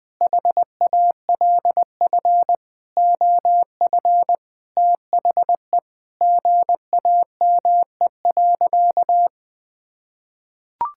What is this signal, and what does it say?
Morse code, heard as a keyed tone, about 20 wpm, 700 Hz. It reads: HALF OF THE GAME.
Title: Sentences with 6-Word Phrases - 20wpm